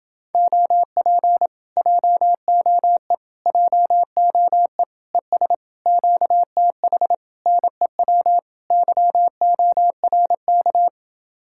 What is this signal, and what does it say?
Morse code, heard as a keyed tone, about 31 wpm, 700 Hz. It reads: OP JOE JOE ES QTH NEW YORK